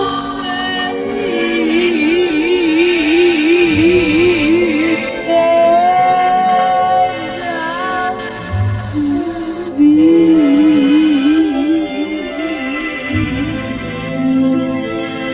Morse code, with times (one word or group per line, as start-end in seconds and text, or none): none